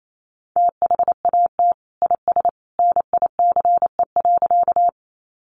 Morse code, heard as a keyed tone, about 28 wpm, 700 Hz. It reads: THAT IS NICE.
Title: Sentences from Top 1100 Words 28wpm